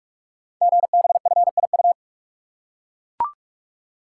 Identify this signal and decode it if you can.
Morse code, heard as a keyed tone, about 44 wpm, 690 Hz. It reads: GBFIU
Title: Code Groups - 5 Letters - 45wpm